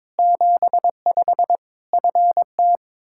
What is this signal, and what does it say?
Morse code, heard as a keyed tone, about 22 wpm, 700 Hz. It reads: 75 FT
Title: Practice QSO - Exchange 2 - Info - 22wpm